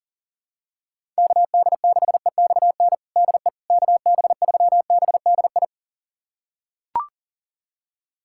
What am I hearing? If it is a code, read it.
Morse code, heard as a keyed tone, about 37 wpm, 700 Hz. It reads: KD6EXN DE KB3BDI
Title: USA Callsign DE Callsign 40wpm